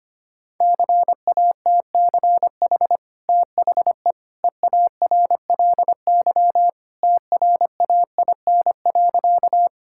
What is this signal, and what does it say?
Morse code, heard as a keyed tone, about 25 wpm, 700 Hz. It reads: CATCH THE EARLY TRAIN.
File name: Easy Sentences with Hints - 25wpm